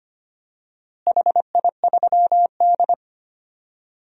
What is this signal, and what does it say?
Morse code, heard as a keyed tone, about 25 wpm, 700 Hz. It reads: HI3D
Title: DXCC-100 Call Signs - 4 Characters - Repeated 3x - 25wpm